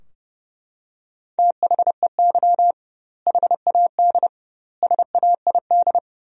Morse code, heard as a keyed tone, about 30 wpm, 700 Hz. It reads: THEY HAD SAID
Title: 2x Word-Spacing - Sentences from Top 400 Words - 30wpm